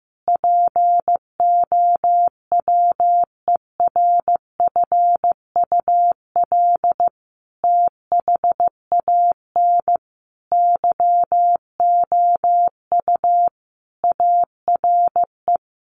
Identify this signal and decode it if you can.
Morse code, heard as a keyed tone, about 15 wpm, 700 Hz. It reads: POWERFUL THAN YOU ARE
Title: Sentences from Top 1300 Words 15wpm